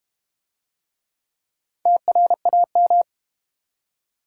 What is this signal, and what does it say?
Morse code, heard as a keyed tone, about 29 wpm, 700 Hz. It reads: TRAM
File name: Easy Words - Speed-Racing with Hints - 25wpm